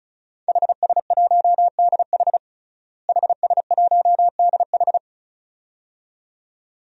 Morse code, heard as a keyed tone, about 34 wpm, 700 Hz. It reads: HS1DH HS1DH
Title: DXCC-100 Call Signs - 5 Characters - Repeated 2x - 35wpm